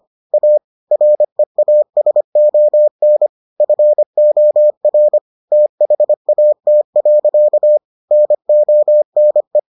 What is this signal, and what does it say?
Morse code, heard as a keyed tone, about 25 wpm, 600 Hz. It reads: A REASON FOR THAT. NONE